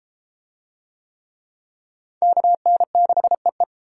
Morse code, heard as a keyed tone, about 31 wpm, 700 Hz. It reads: KN6EE